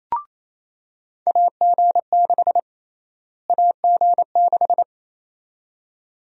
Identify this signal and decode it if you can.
Morse code, heard as a keyed tone, about 28 wpm, 700 Hz. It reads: AG6 AG6
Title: USA Call Sign Prefixes - Repeated 2x - 28wpm